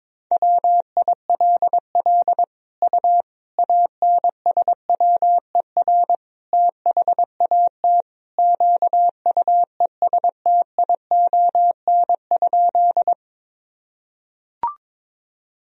Morse code, heard as a keyed tone, about 22 wpm, 700 Hz. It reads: WILL U ANSWER THAT QUESTION?
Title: Sentences from Top 400 Words and Abbreviations 22wpm